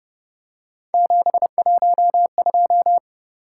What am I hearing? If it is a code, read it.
Morse code, heard as a keyed tone, about 29 wpm, 700 Hz. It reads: 712